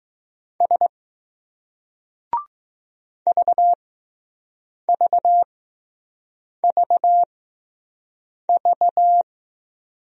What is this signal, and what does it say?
Morse code, heard as a keyed tone, about 20 wpm, spaced as for 6 wpm, 700 Hz. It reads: S VVVV